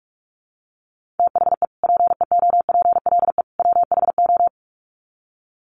Morse code, heard as a keyed tone, about 44 wpm, 700 Hz. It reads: THE PEOPLE WHO
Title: Sets of 3 Words 45wpm